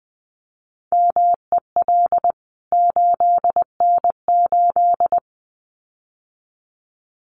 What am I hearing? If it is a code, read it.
Morse code, heard as a keyed tone, about 20 wpm, 700 Hz. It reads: MEL 8N8
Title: CWT Exchange 20wpm